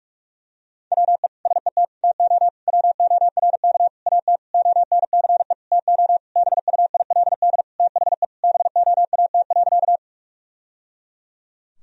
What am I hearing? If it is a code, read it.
Morse code, heard as a keyed tone, about 46 wpm, 700 Hz. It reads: WE SET TO WORK AT ONCE TO BUILD THE BOAT.
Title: Sentences from Top 1000 Words - Review Part 1 - 45wpm